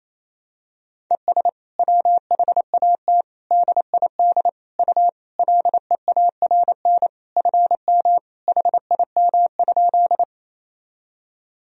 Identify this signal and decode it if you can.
Morse code, heard as a keyed tone, about 28 wpm, 700 Hz. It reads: ES WHAT DID U LEARN FM HIM?
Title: Sentences from Top 300 Words and Abbreviations 28wpm